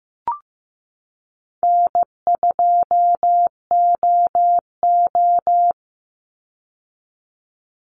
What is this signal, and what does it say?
Morse code, heard as a keyed tone, about 15 wpm, 700 Hz. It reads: N2OO
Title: Call Signs Repeated 3x - 15wpm